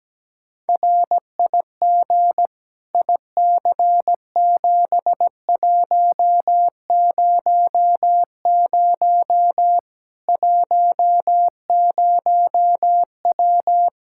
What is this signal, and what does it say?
Morse code, heard as a keyed tone, about 17 wpm, 700 Hz. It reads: RIG IC7100 10W
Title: Practice QSO - Exchange 2 - Info - 17wpm